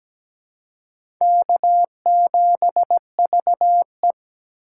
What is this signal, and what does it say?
Morse code, heard as a keyed tone, about 17 wpm, 700 Hz. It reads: K7VE
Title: USA Call Signs - 4x Character Spacing - 17wpm